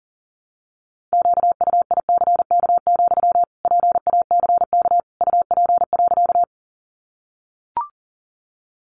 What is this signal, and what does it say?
Morse code, heard as a keyed tone, about 40 wpm, 700 Hz. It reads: QUICK, PACK UP.
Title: Sentences from Top 1300 Words 40wpm